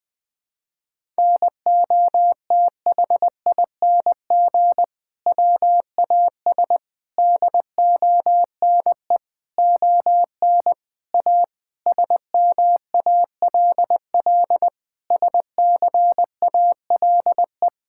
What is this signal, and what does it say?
Morse code, heard as a keyed tone, about 20 wpm, 700 Hz. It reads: NOTHING WAS DONE ON A SMALL SCALE
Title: Sentences from Top 1000 Words - Review Part 5 - 20wpm